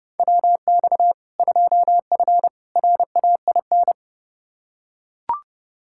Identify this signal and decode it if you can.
Morse code, heard as a keyed tone, about 30 wpm, 700 Hz. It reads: WX 2F RAIN
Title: K1USN SST QSO Practice 30wpm